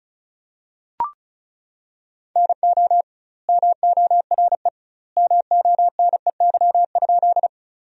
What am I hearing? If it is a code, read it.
Morse code, heard as a keyed tone, about 36 wpm, 700 Hz. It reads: NO MORE MONEY?
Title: 2x Word-Spacing - Sentences from Top 400 Words - 35wpm